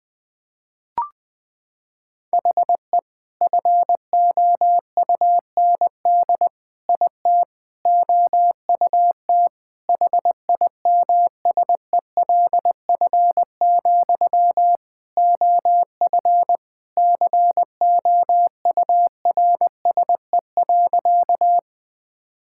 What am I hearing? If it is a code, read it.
Morse code, heard as a keyed tone, about 20 wpm, 700 Hz. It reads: HE FOUND IT OUT HIMSELF, OF COURSE.
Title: Sentences from Top 1600 Words 20wpm